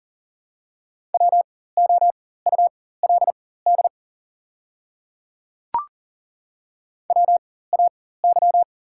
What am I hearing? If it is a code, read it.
Morse code, heard as a keyed tone, about 38 wpm, spaced as for 18 wpm, 690 Hz. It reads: WOULD WAY
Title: 4x Character Spacing - Top 100 Words - 40wpm